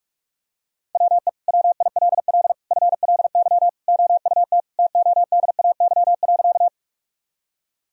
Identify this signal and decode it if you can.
Morse code, heard as a keyed tone, about 46 wpm, 700 Hz. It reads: WE WILL FLY OUT TODAY.
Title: Sentences from Top 1000 Words - Review Part 2 - 45wpm